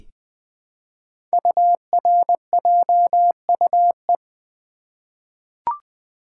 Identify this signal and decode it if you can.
Morse code, heard as a keyed tone, about 20 wpm, 700 Hz. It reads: URJUE
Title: Code Groups - 5 Letters - 20wpm